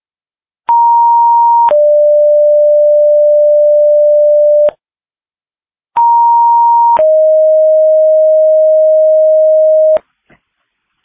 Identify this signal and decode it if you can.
Two-tone page, heard as a one-second tone, then a three-second tone, about 940 then 590 Hz, about 940 then 620 Hz. It is heard from 0.7 to 10.0 s.